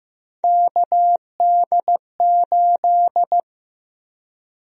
Morse code, heard as a keyed tone, about 15 wpm, 700 Hz. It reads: KD8
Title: USA Call Sign Prefixes - Repeated 2x - 15wpm